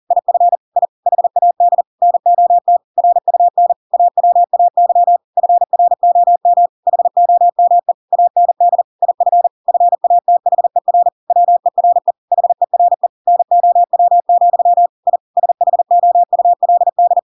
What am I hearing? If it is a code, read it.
Morse code, heard as a keyed tone, about 40 wpm, 700 Hz. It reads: IF I HAD NOT RUN AWAY FROM HOME AND IF FATHER WERE HERE NOW, I SHOULD
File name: Pinocchio / Pinocchio ch5 40 40 ik1hge